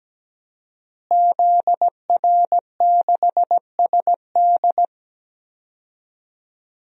Morse code, heard as a keyed tone, about 17 wpm, 700 Hz. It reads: ZR6SD
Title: DXCC-100 Call Signs - 5 Characters - Repeated 3x - 17wpm